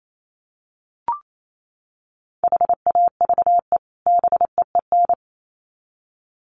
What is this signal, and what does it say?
Morse code, heard as a keyed tone, about 28 wpm, 700 Hz. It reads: HAVE BEEN
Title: Sets of 2 Words 28wpm